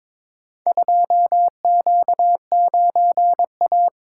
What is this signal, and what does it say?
Morse code, heard as a keyed tone, about 22 wpm, 700 Hz. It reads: 2Q9A
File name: practice / 22wpm / Four Letter-Number Combinations - 22wpm